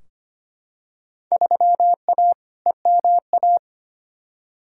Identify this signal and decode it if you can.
Morse code, heard as a keyed tone, about 25 wpm, 700 Hz. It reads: 3A EMA
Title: Field Day Exchanges - 25wpm